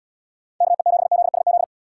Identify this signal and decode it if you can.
Morse code, heard as a keyed tone, about 70 wpm, 700 Hz. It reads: KE7ZSZ